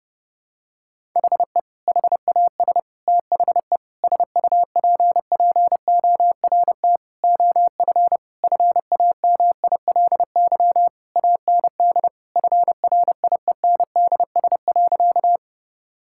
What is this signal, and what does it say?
Morse code, heard as a keyed tone, about 32 wpm, 700 Hz. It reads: HE HAS THE SUPPORT OF FAMILY AND FRIENDS.
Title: Sentences from Top 1000 Words - Review Part 5 - 30wpm